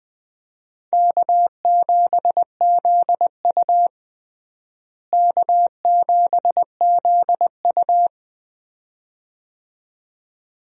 Morse code, heard as a keyed tone, about 20 wpm, 700 Hz. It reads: K7ZU K7ZU
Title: USA Call Signs - 4 Characters - Repeated 2x - 20wpm